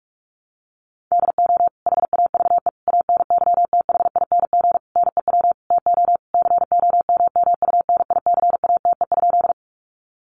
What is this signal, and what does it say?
Morse code, heard as a keyed tone, about 44 wpm, 700 Hz. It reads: DO HAVE ANYTHING NEW TO COMMUNICATE?
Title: Sentences from Top 1300 Words 45wpm